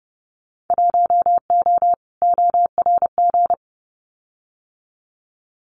Morse code, heard as a keyed tone, about 30 wpm, 700 Hz. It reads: 1O ORG
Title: Winter Field Day Exchanges 30wpm